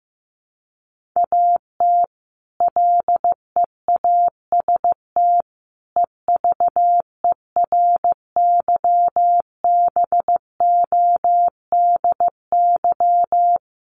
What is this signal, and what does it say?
Morse code, heard as a keyed tone, about 15 wpm, 700 Hz. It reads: AT LEAST EVERYBODY
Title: Sentences from Top 1400 Words 15wpm